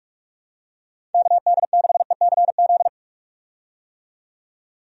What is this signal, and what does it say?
Morse code, heard as a keyed tone, about 46 wpm, 700 Hz. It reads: KD6ECZ
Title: USA Call Signs - 6 Characters - 45wpm